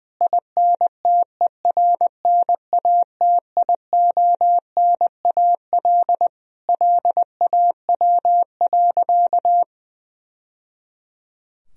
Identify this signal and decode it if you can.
Morse code, heard as a keyed tone, about 20 wpm, 700 Hz. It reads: INTERNATIONAL LAW.
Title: Sentences from Top 1000 Words - Review Part 4 - 20wpm